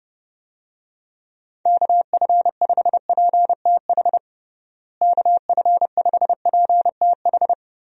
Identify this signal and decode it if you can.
Morse code, heard as a keyed tone, about 32 wpm, 700 Hz. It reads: KF5PTH KF5PTH
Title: USA Call Signs - 6 Characters - Repeated 2x - 30wpm